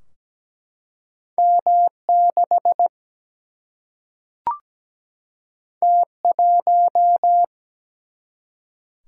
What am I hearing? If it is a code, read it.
Morse code, heard as a keyed tone, about 17 wpm, 700 Hz. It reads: M6 T1
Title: Single-Letter Single-Number 17wpm